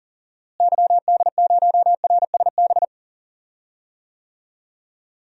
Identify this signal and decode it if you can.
Morse code, heard as a keyed tone, about 40 wpm, 700 Hz. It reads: YD0RSB
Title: DXCC-100 Call Signs - 6 Characters - Repeated 3x - 40wpm